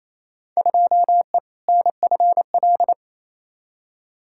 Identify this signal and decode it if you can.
Morse code, heard as a keyed tone, about 28 wpm, 700 Hz. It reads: 2E NFL
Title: Field Day Exchanges Encore - 28wpm